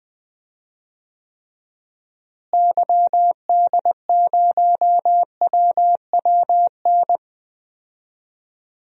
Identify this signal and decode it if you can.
Morse code, heard as a keyed tone, about 20 wpm, 700 Hz. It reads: YD0WWN
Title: DXCC-100 Call Signs - 6 Characters - Repeated 3x - 20wpm